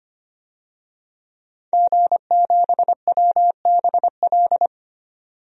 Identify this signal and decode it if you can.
Morse code, heard as a keyed tone, about 25 wpm, 700 Hz. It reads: G7WBL
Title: DXCC-100 Call Signs - 5 Characters - 25wpm